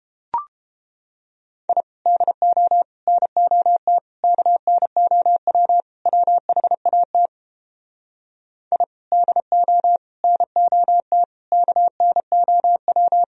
Morse code, heard as a keyed tone, about 31 wpm, 700 Hz. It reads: I DO NOT KNOW WHAT I DO NOT KNOW